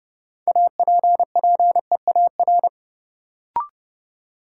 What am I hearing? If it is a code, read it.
Morse code, heard as a keyed tone, about 32 wpm, 700 Hz. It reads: APPEAR
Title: Sentences from Top 1200 Words - New Word Drill Alt - 30wpm